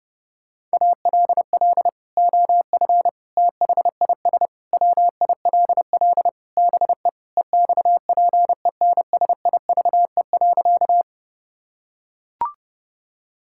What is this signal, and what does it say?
Morse code, heard as a keyed tone, about 30 wpm, 700 Hz. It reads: ALL OF THIS WILL BE EXPENSIVE.
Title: Sentences from Top 1000 Words - Review Part 6 - 30wpm